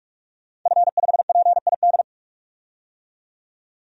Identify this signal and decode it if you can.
Morse code, heard as a keyed tone, about 46 wpm, 700 Hz. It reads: F5PID
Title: DXCC-100 Call Signs - 5 Characters - 45wpm